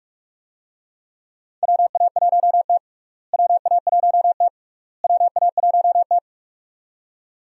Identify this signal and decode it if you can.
Morse code, heard as a keyed tone, about 46 wpm, 700 Hz. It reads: WA1T WA1T WA1T
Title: USA Call Signs - 4 Characters - Repeated 3x - 45wpm